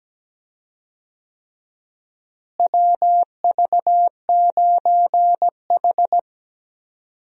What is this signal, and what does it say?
Morse code, heard as a keyed tone, about 17 wpm, 700 Hz. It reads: WV9H
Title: USA Call Signs - 4 Characters - Repeated 3x - 17wpm